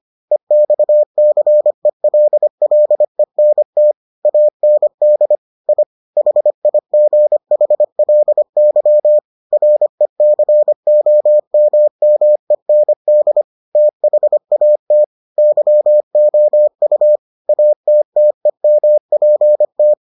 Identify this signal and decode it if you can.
Morse code, heard as a keyed tone, about 25 wpm, 600 Hz. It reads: EXCELLENT AND I HIGHLY RECOMMEND THAT YOU ATTEMPT